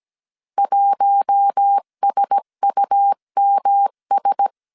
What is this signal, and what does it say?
Morse code, heard as a keyed tone, about 17 wpm, 800 Hz. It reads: 1SUMS
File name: CW / DET / 1 6 1500C S C 16wpm 2733